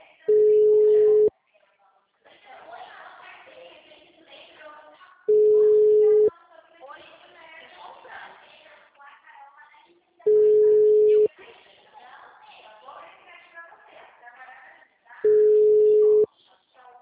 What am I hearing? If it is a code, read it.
Ringback tone (Europe, 425 Hz), with speech.
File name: out- -2007- - - .37737